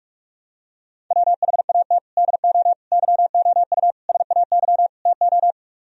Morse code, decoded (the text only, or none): WHAT DO YOU SAY TO